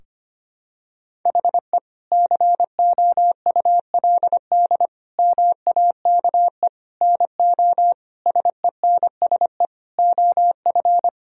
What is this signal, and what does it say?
Morse code, heard as a keyed tone, about 25 wpm, 700 Hz. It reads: HE COULD MAKE NO SENSE OF
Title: Sentences from Top 1000 Words - Review Part 4 - 25wpm